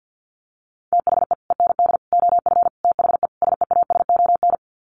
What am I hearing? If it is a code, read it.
Morse code, heard as a keyed tone, about 50 wpm, 700 Hz. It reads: THE END OF THE SEASON
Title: Sets of 5 Words Encore 50wpm